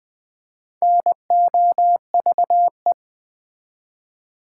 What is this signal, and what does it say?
Morse code, heard as a keyed tone, about 20 wpm, 700 Hz. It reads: NOVE
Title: Build A Word - 4 Letters At A Time - 8-Letter Words - 20wpm